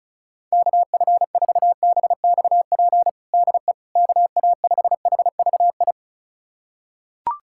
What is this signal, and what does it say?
Morse code, heard as a keyed tone, about 36 wpm, 700 Hz. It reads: KF4BXP DE KA5HVI
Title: USA Callsign DE Callsign - 6 Characters - 35wpm